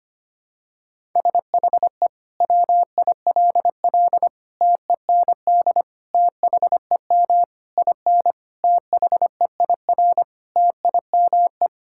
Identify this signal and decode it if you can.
Morse code, heard as a keyed tone, about 25 wpm, 700 Hz. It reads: SHE WILL TEND THEM IN THEIR TIME
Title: Sentences from Top 1000 Words - Review Part 5 - 25wpm